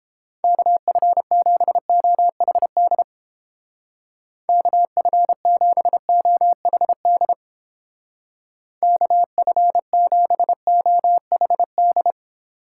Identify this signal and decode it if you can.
Morse code, heard as a keyed tone, about 30 wpm, 700 Hz. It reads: KF7OHD KF7OHD KF7OHD